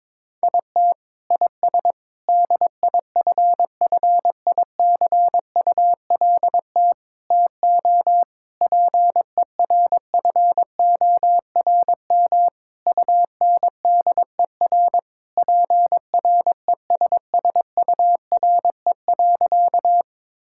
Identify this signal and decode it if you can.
Morse code, heard as a keyed tone, about 22 wpm, 700 Hz. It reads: IT IS DIFFICULT TO PERFORM UNDER PRESSURE.
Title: Sentences from Top 1000 Words - Review Part 6 - 22wpm